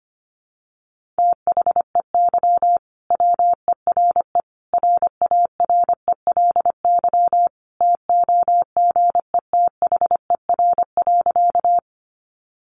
Morse code, heard as a keyed tone, about 25 wpm, 700 Hz. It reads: THEY WERE RARELY TOGETHER.
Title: Sentences from Top 1400 Words 25wpm